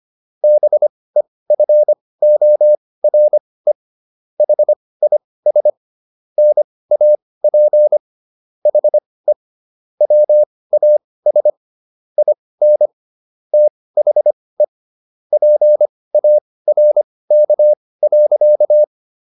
Morse code, heard as a keyed tone, about 25 wpm, spaced as for 18 wpm, 600 Hz. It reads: BEFORE HIS NAP HE WAS IN THE PARK.